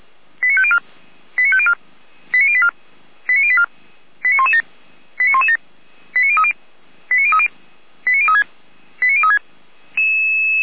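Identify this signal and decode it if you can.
Five-tone selective call (ZVEI), heard as quick runs of five tones, tones of about 70 ms. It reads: 89504 89504 89084 89084 89117 89117 89030 89030 89036 89036
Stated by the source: Five-tone ZVEI selective call received on an SDR.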